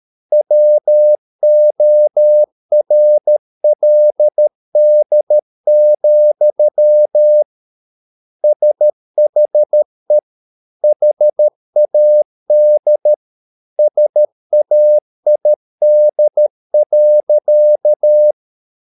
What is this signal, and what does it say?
Morse code, heard as a keyed tone, about 13 wpm, 600 Hz. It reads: WORLD, SHE HAD SAID.